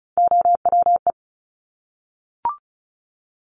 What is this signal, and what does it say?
Morse code, heard as a keyed tone, about 36 wpm, 700 Hz. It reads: OWE
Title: Top Three Letter Combinations in Words 35wpm